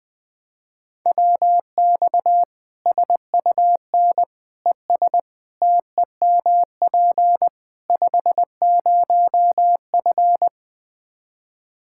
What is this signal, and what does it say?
Morse code, heard as a keyed tone, about 20 wpm, 700 Hz. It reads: WX SUN ES TEMP 50F
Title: Practice QSO - Exchange 2 - Info - 20wpm